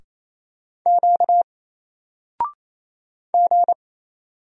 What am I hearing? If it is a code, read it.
Morse code, heard as a keyed tone, about 27 wpm, 700 Hz. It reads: Q G
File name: Single-Letter-Number-28wpm